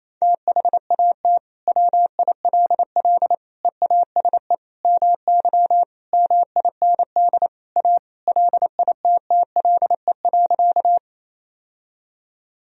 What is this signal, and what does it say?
Morse code, heard as a keyed tone, about 28 wpm, 700 Hz. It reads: THAT WILL EASE MY MIND A LITTLE.
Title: Sentences from Top 1000 Words - Review Part 1 - 28wpm